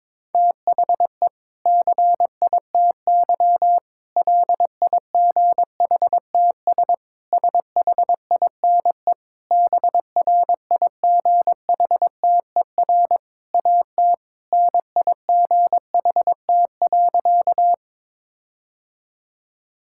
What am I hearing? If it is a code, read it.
Morse code, heard as a keyed tone, about 22 wpm, 700 Hz. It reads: THE CITY LIGHTS SHINE BRIGHTER AT NIGHT.
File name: Difficult Sentences with Hints - 22wpm